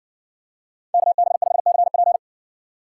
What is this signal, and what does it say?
Morse code, heard as a keyed tone, about 56 wpm, 700 Hz. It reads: KB5CP